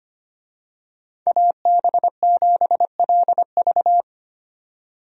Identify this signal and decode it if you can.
Morse code, heard as a keyed tone, about 25 wpm, 700 Hz. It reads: AB7LV
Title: USA Call Signs - 5 Characters - 25wpm